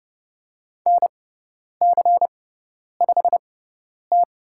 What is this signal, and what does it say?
Morse code, heard as a keyed tone, about 30 wpm, spaced as for 10 wpm, 700 Hz. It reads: NC5T
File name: Lesson 18g - Letter m - Call Signs 30wpm